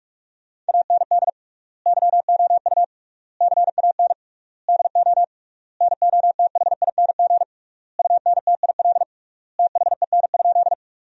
Morse code, heard as a keyed tone, about 42 wpm, 700 Hz. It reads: AND YOU CAN DO NOTHING UNTIL THEN?
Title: 3x Word-Spacing - Sentences from Top 500 Words - 45wpm